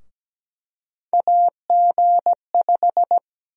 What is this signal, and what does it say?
Morse code, heard as a keyed tone, about 17 wpm, 700 Hz. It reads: AG5
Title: USA Call Sign Prefixes - Repeated 2x - 17wpm